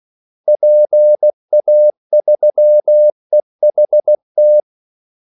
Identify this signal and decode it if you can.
Morse code, heard as a keyed tone, about 16 wpm, 600 Hz. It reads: PA3EHT
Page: Kaldesignaler 16 wpm